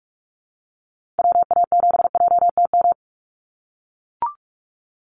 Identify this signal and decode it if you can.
Morse code, heard as a keyed tone, about 44 wpm, 700 Hz. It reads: WA7JTM
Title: NA SOTA Activator Callsigns 45wpm